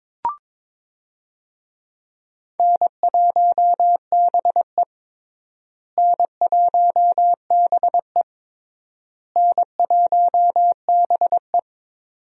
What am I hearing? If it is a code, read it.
Morse code, heard as a keyed tone, about 22 wpm, 700 Hz. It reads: N1BE N1BE N1BE
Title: USA Call Signs - 4 Characters - Repeated 3x - 22wpm